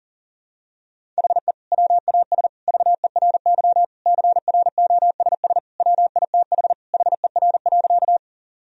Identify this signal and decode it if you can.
Morse code, heard as a keyed tone, about 40 wpm, 700 Hz. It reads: HE WAS VERY CROSS WITH HER.
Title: Sentences from Top 1000 Words - Review Part 1 - 40wpm